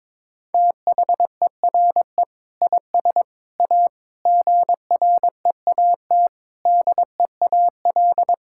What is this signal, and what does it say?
Morse code, heard as a keyed tone, about 21 wpm, 700 Hz. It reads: THERE IS A GREAT DEAL
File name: practice / 22wpm / Sets of 6 Words - 22wpm